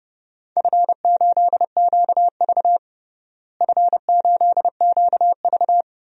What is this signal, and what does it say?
Morse code, heard as a keyed tone, about 30 wpm, 700 Hz. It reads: F8QV F8QV
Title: DXCC-100 Call Signs - 4 Characters - Repeated 3x - 30wpm